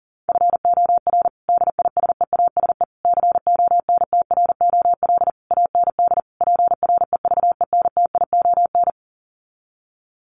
Morse code, heard as a keyed tone, about 37 wpm, 700 Hz. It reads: FOR DISEASE CONTROL AND PREVENTION
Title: Sets of 5 Words Encore2 40wpm